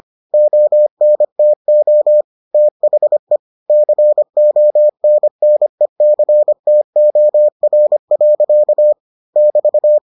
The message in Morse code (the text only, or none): ONTO THE CONNECTOR. <BT>